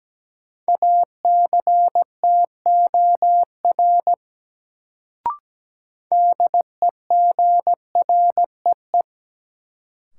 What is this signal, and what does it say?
Morse code, heard as a keyed tone, about 17 wpm, 700 Hz. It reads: ACTOR DEGREE
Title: Sentences from Top 1500 Words - New Word Drill - 17wpm